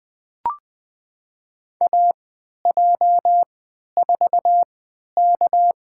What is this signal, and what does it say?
Morse code, heard as a keyed tone, about 21 wpm, spaced as for 12 wpm, 700 Hz. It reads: AJ4K
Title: USA Call Signs - 4 Characters - 3x Character Spacing - 20wpm